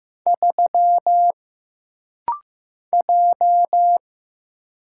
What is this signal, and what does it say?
Morse code, heard as a keyed tone, about 15 wpm, 700 Hz. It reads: 3 J
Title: Single Letter-Number Alt - Rapid-Fire 15wpm